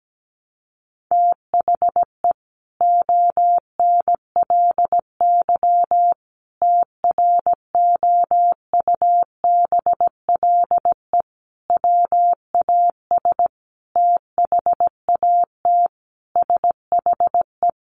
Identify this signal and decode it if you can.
Morse code, heard as a keyed tone, about 17 wpm, 700 Hz. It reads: THE ONLY TROUBLE WAS THAT SHE
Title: Sentences from Top 1200 Words 17wpm